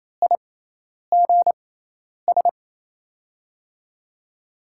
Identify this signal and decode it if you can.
Morse code, heard as a keyed tone, about 28 wpm, spaced as for 10 wpm, 700 Hz. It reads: IGS
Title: USA Call Signs - 6 Characters - 6x Character Spacing - 28wpm